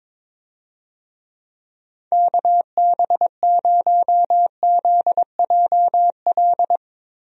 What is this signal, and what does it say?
Morse code, heard as a keyed tone, about 22 wpm, 700 Hz. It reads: KB0ZJL
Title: USA Call Signs - 6 Characters - Repeated 2x - 22wpm